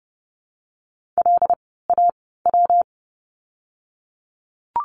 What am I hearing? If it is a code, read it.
Morse code, heard as a keyed tone, about 30 wpm, 700 Hz. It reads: L A W
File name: course / 30:10wpm / 115 Lesson 19f - Letter w - Words 30wpm